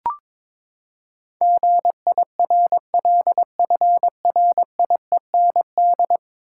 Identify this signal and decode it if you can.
Morse code, heard as a keyed tone, about 22 wpm, 700 Hz. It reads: GIRLFRIEND
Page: morse-code-ninja-practice-22wpm